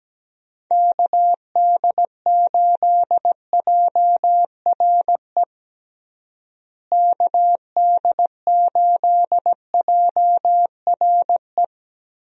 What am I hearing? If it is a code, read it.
Morse code, heard as a keyed tone, about 17 wpm, 700 Hz. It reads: KD8JRE KD8JRE
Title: USA Call Signs - 6 Characters - Repeated 3x - 17wpm